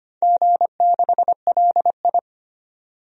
Morse code, heard as a keyed tone, about 25 wpm, 700 Hz. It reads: G6LI
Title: DXCC-100 Call Signs - 4 Characters - 25wpm